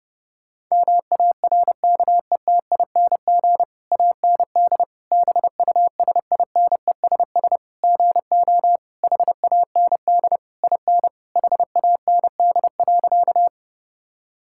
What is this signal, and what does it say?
Morse code, heard as a keyed tone, about 29 wpm, 700 Hz. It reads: MARKETING AND BUSINESS GO HAND IN HAND.
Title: Sentences from Top 1000 Words - Review Part 5 - 30wpm